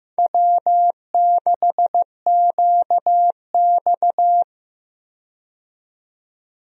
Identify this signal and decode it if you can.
Morse code, heard as a keyed tone, about 15 wpm, 700 Hz. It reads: W6QX
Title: USA Call Signs - 4 Characters - 15wpm